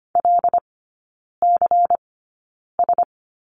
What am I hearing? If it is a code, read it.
Morse code, heard as a keyed tone, about 25 wpm, spaced as for 9 wpm, 700 Hz. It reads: LCS
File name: course / 25:10wpm / 093 Lesson 16e - Letter c - Three Characters 25wpm